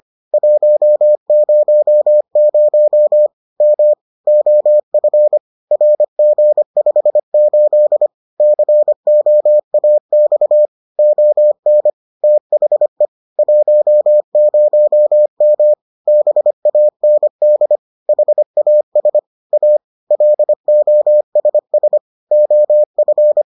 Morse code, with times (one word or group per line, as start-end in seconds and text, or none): 0.3-3.3 100
3.6-3.9 M
4.3-5.4 OF
5.7-8.1 RG58
8.4-10.7 COAX
11.0-11.9 ON
12.2-13.1 THE
13.4-15.7 10M
16.1-17.8 BAND
18.1-19.2 HAS
19.5-19.8 A
20.1-22.0 LOSS
22.3-23.4 OF